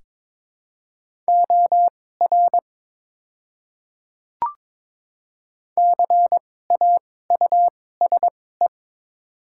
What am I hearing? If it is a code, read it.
Morse code, heard as a keyed tone, about 22 wpm, spaced as for 16 wpm, 700 Hz. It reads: OR CAUSE
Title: 2x Character Spacing - Top 200 Words - 22wpm